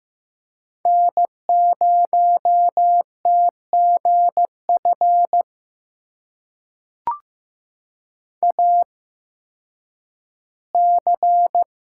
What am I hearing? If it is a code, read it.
Morse code, heard as a keyed tone, about 15 wpm, spaced as for 10 wpm, 700 Hz. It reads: N0TGF AC
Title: USA Call Signs - 5 Characters - 8x Character Spacing - 15wpm